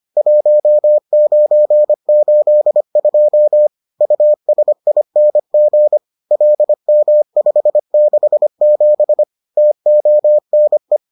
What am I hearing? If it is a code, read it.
Morse code, heard as a keyed tone, about 25 wpm, 600 Hz. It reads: 1982 USING LM567 TONE